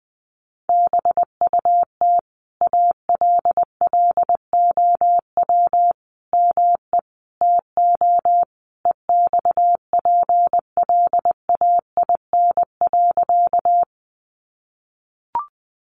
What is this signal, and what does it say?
Morse code, heard as a keyed tone, about 20 wpm, 700 Hz. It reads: BUT ALLOW ME TO EXPLAIN.
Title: Sentences from Top 1100 Words 20wpm